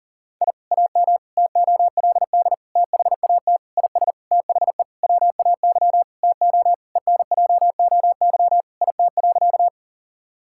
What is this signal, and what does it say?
Morse code, heard as a keyed tone, about 40 wpm, 700 Hz. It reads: I AM TOLD THAT IS THE WAY TO ENJOY IT.
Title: Sentences from Top 1000 Words - Review Part 5 - 40wpm